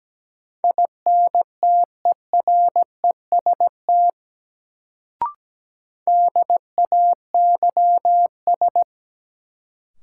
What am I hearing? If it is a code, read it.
Morse code, heard as a keyed tone, about 17 wpm, 700 Hz. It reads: INTEREST DAYS
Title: Sets of 5 Words Encore - New Word Drill - 17wpm